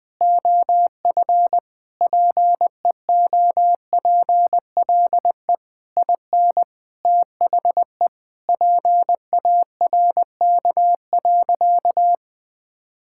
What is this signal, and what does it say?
Morse code, heard as a keyed tone, about 20 wpm, 700 Hz. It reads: OF PEOPLE IN THE PARK.
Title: Sentences with 5-Word Phrases - 20wpm